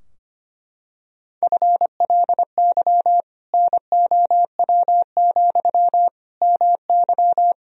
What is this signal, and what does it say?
Morse code, heard as a keyed tone, about 25 wpm, 700 Hz. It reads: FLY NOW, MY
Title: Sentences from Top 1000 Words - Review Part 2 - 25wpm